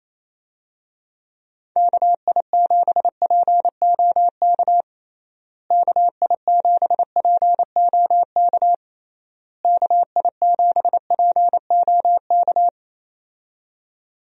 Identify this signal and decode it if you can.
Morse code, heard as a keyed tone, about 28 wpm, 700 Hz. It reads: KI7POK KI7POK KI7POK
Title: USA Call Signs - 6 Characters - Repeated 3x - 28wpm